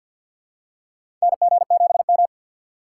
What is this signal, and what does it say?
Morse code, heard as a keyed tone, about 53 wpm, 700 Hz. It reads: NG7M
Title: USA Call Signs - 2x Character Spacing - 50wpm